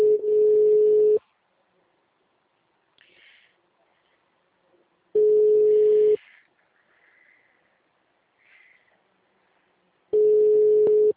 Ringback tone (Europe, 425 Hz).
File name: out- -2010- - - .42